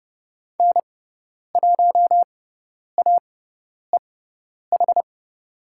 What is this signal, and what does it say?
Morse code, heard as a keyed tone, about 30 wpm, spaced as for 10 wpm, 700 Hz. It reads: N1AEH
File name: Lesson 18g - Letter m - Call Signs 30wpm